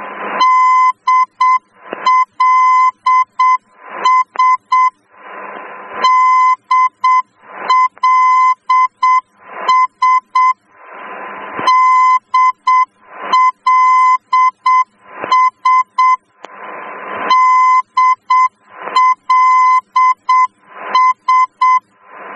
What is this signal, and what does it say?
Morse code, heard as a keyed tone, about 7 wpm, 1020 Hz. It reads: DLS DLS DLS DLS